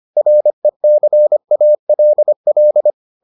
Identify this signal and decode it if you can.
Morse code, heard as a keyed tone, about 25 wpm, 600 Hz. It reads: RECALL